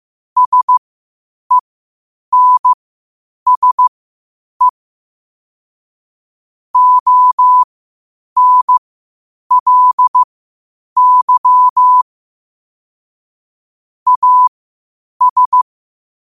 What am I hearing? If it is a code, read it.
Morse code, heard as a keyed tone, about 15 wpm, spaced as for 8 wpm, 1000 Hz. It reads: SENSE ONLY AS
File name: qotdfiles / 8wpm